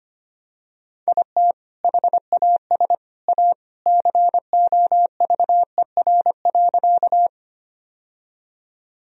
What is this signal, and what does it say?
Morse code, heard as a keyed tone, about 25 wpm, 700 Hz. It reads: IT HAS A COVER.
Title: Sentences from Top 1000 Words - Review Part 1 - 25wpm